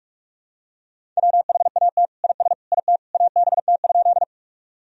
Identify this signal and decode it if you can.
Morse code, heard as a keyed tone, about 46 wpm, 700 Hz. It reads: WHAT IS IT ABT?
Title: Sentences from Top 100 Words and Abbreviations 45wpm